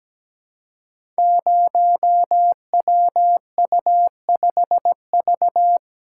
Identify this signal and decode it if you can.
Morse code, heard as a keyed tone, about 17 wpm, 700 Hz. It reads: 0WU5V
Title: Code Groups - 5 Letter:Numbers - 17wpm